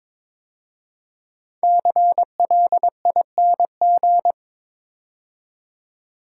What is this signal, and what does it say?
Morse code, heard as a keyed tone, about 23 wpm, 700 Hz. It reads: CLING
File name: 5 Letter Words - First 3 Letters Given - 22wpm